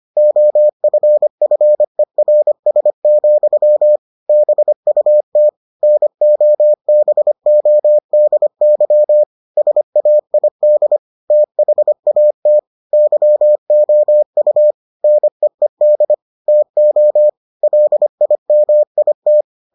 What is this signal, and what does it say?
Morse code, heard as a keyed tone, about 25 wpm, 600 Hz. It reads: OFFERS, BUT NOBODY SAID THAT YOU NEED TO LIMIT